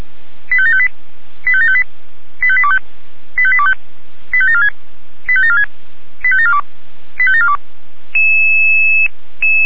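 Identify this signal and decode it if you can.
Five-tone selective call (ZVEI), heard as quick runs of five tones, tones of about 70 ms. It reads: 85758 85758 85825 85825 85746 85746 85742 85742
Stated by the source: ZVEI five-tone selective call received on SDR.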